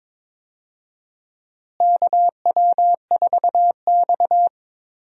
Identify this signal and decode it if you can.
Morse code, heard as a keyed tone, about 22 wpm, 700 Hz. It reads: KW4X
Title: USA Call Signs - 4 Characters - Repeated 2x - 22wpm